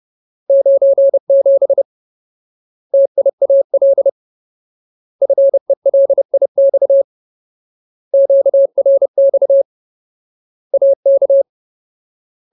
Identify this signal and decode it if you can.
Morse code, heard as a keyed tone, about 30 wpm, 550 Hz. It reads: 97 TIAL FELIX QRX AK
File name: Ultimate Word List / 30WPM Ultimate